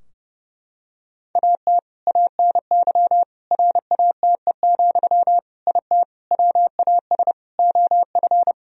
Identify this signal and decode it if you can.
Morse code, heard as a keyed tone, about 30 wpm, 700 Hz. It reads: AT ANY RATE, IT WAS OF
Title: Sentences from Top 1000 Words - Review Part 4 - 30wpm